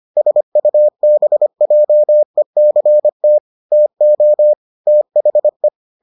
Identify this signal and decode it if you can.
Morse code, heard as a keyed tone, about 25 wpm, 600 Hz. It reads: SUBJECT TO THE